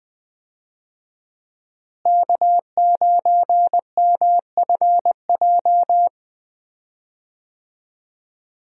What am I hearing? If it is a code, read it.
Morse code, heard as a keyed tone, about 20 wpm, 700 Hz. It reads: K9MFJ
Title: USA Call Signs - 5 Characters - Repeated 3x - 20wpm